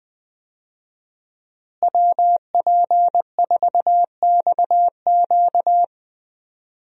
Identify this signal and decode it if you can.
Morse code, heard as a keyed tone, about 20 wpm, 700 Hz. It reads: WP4XQ
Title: USA Call Signs - 5 Characters - Repeated 2x - 20wpm